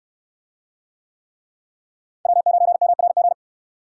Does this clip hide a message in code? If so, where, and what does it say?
Yes: Morse code from 2.3 to 3.3 s, heard as a keyed tone, about 69 wpm, 700 Hz. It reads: W9MRG